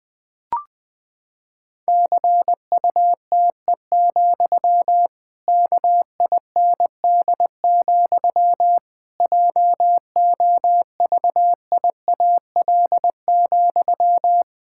Morse code, heard as a keyed tone, about 20 wpm, 700 Hz. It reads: CUTE, KIND, JOVIAL,
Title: Pangram Challenge v2 - 20wpm